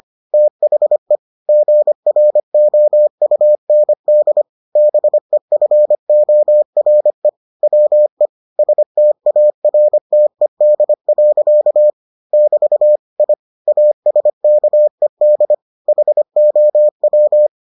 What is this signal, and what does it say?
Morse code, heard as a keyed tone, about 25 wpm, 600 Hz. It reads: THE GROUND BEFORE WE STARTED. <BT> I ASKED HOW